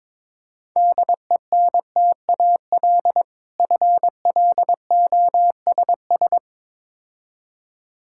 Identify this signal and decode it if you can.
Morse code, heard as a keyed tone, about 22 wpm, 700 Hz. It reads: DENTAL FLOSS
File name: Difficult Words with Hints - 22wpm